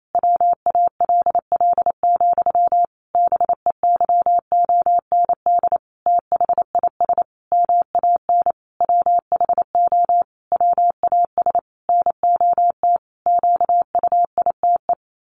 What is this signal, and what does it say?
Morse code, heard as a keyed tone, about 28 wpm, 700 Hz. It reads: WALL, BEYOND THIS MAN WHO WAS NOT QUITE
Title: A Sound Of Thunder 28wpm